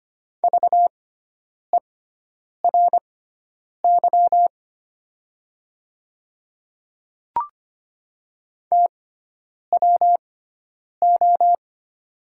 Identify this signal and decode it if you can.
Morse code, heard as a keyed tone, about 25 wpm, spaced as for 9 wpm, 700 Hz. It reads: VERY TWO
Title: 6x Character Spacing - Top 200 Words - 25wpm